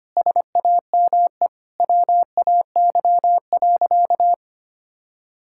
Morse code, heard as a keyed tone, about 25 wpm, 700 Hz. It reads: SAME WAY.
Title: Sentences from Top 1600 Words 25wpm